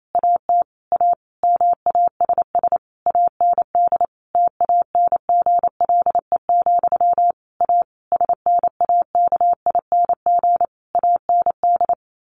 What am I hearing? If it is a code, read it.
Morse code, heard as a keyed tone, about 28 wpm, 700 Hz. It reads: AT A MASS AND TANGLE, A SNAKING AND
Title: A Sound Of Thunder 28wpm